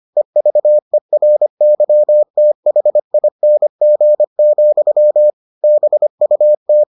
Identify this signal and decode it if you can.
Morse code, heard as a keyed tone, about 25 wpm, 600 Hz. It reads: EVERYTHING, BUT